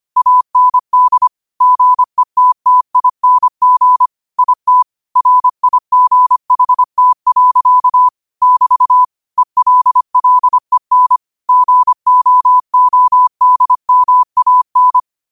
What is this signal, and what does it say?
Morse code, heard as a keyed tone, about 25 wpm, 1000 Hz. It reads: AND GETTING IT RIGHT. <BT> ELLEN GOODMAN
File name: qotdfiles / 25wpm